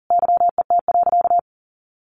Morse code, heard as a keyed tone, about 40 wpm, 700 Hz. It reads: YET.
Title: A Sound Of Thunder 40wpm